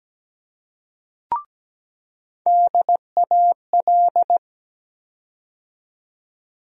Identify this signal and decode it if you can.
Morse code, heard as a keyed tone, about 17 wpm, 700 Hz. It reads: DAL
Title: USA Call Sign Postfixes - 3 Letters 17wpm